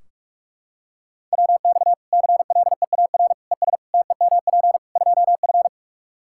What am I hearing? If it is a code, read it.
Morse code, heard as a keyed tone, about 46 wpm, 700 Hz. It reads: WX CLEAR ES TEMP 2F
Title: Practice QSO - Exchange 2 - Info - 45wpm